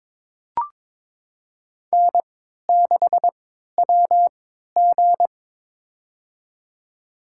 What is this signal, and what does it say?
Morse code, heard as a keyed tone, about 22 wpm, spaced as for 13 wpm, 700 Hz. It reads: N6WG
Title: USA Call Signs - 4 Characters - 3x Character Spacing - 22wpm